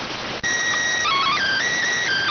SSTV (slow-scan television), PD 290.